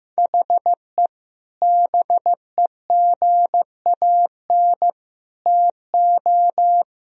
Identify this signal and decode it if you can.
Morse code, heard as a keyed tone, about 15 wpm, 700 Hz. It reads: HE BEGAN TO
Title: Sentences from Top 1000 Words - Review Part 1 - 15wpm